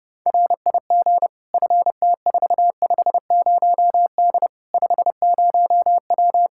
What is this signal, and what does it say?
Morse code, heard as a keyed tone, about 30 wpm, 700 Hz. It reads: RIG FT450D 50W
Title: Practice QSO - Exchange 2 - Info - 30wpm